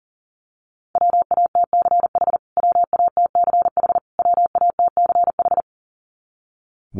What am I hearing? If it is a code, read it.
Morse code, heard as a keyed tone, about 38 wpm, 700 Hz. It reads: WATCH WATCH WATCH
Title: Top 300 Words Repeated 40wpm